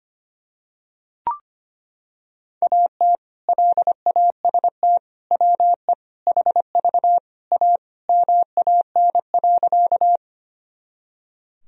Morse code, heard as a keyed tone, about 25 wpm, 700 Hz. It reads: AT LAST WE HV A MAN.
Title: Sentences from Top 300 Words and Abbreviations 25wpm